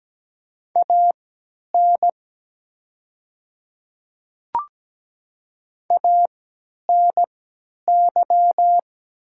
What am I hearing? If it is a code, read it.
Morse code, heard as a keyed tone, about 17 wpm, spaced as for 10 wpm, 700 Hz. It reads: AN ANY
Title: 3x Character Spacing - Top 200 Words - 17wpm